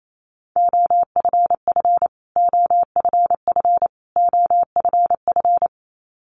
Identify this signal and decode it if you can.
Morse code, heard as a keyed tone, about 28 wpm, 700 Hz. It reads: OFF OFF OFF
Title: Top 300 Words Repeated 28wpm